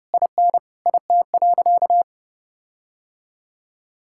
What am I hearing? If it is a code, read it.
Morse code, heard as a keyed tone, about 32 wpm, 700 Hz. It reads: IN IT.
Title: Sentences from Top 1600 Words 30wpm